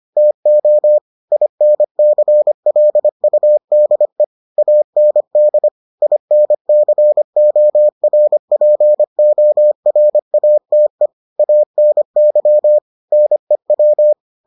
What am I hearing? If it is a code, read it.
Morse code, heard as a keyed tone, about 25 wpm, 600 Hz. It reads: TO INCLUDE AND INCORPORATE ANY NEW